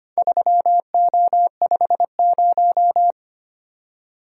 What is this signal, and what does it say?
Morse code, heard as a keyed tone, about 25 wpm, 700 Hz. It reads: 3O50
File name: practice / 25wpm / Four Letter-Number Combinations - 25wpm